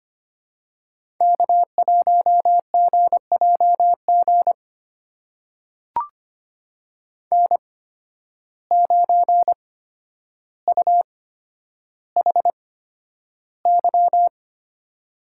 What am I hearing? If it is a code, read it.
Morse code, heard as a keyed tone, about 25 wpm, spaced as for 11 wpm, 700 Hz. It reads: K1GJG N9UHY
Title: USA Call Signs - 5 Characters - 8x Character Spacing - 25wpm